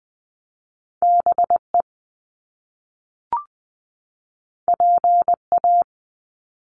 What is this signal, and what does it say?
Morse code, heard as a keyed tone, about 20 wpm, 700 Hz. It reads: B E P A
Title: Top Two Letter Combinations in Words 20wpm